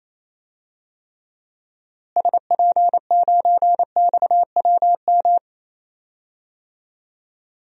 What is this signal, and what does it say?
Morse code, heard as a keyed tone, about 28 wpm, 700 Hz. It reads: SP9XWM
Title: DXCC-100 Call Signs - 6 Characters - Repeated 3x - 28wpm